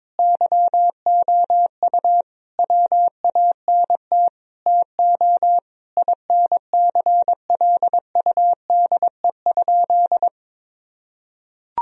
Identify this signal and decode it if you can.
Morse code, heard as a keyed tone, about 21 wpm, 700 Hz. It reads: YOU WANT TO INCLUDE?
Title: Sentences from Top 1000 Words - Review Part 3 - 22wpm